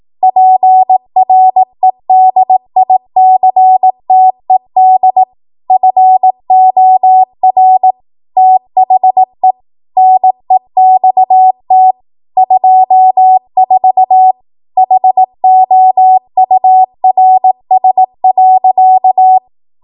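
Morse code, heard as a keyed tone, about 18 wpm, 750 Hz. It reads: PREDICTED FOR THE NEXT 24 HOURS.
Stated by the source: Code practice at 18 WPM, a transition run bridging 15 to 20 WPM.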